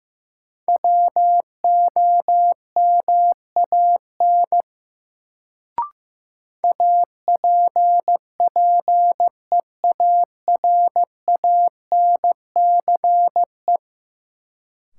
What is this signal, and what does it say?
Morse code, heard as a keyed tone, about 15 wpm, 700 Hz. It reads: WOMAN APPEARANCE
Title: Sentences from Top 1100 Words - New Word Drill Alt - 15wpm